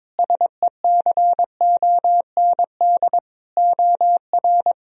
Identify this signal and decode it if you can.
Morse code, heard as a keyed tone, about 22 wpm, 700 Hz. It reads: SECOND OR
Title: Sentences from Top 1000 Words - Review Part 1 - 22wpm